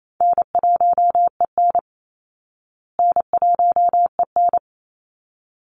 Morse code, heard as a keyed tone, about 28 wpm, 700 Hz. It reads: N1EN N1EN
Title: Call Signs Repeated 3x - 28wpm